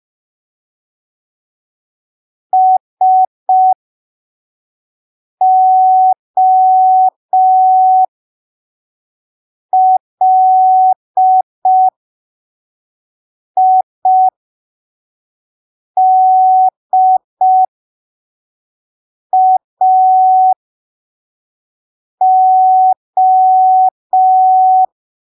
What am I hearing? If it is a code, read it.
Morse code, heard as a keyed tone, about 5 wpm, spaced as for 3 wpm, 750 Hz. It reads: SOLIDAO
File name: arquivos / texto / texto 23 05ppm